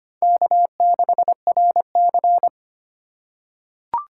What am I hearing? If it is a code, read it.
Morse code, heard as a keyed tone, about 25 wpm, 700 Hz. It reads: K6RC
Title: K1USN SST QSO Practice - 25wpm